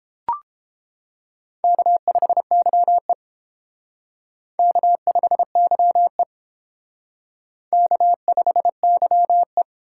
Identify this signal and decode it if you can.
Morse code, heard as a keyed tone, about 30 wpm, 700 Hz. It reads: K5YE K5YE K5YE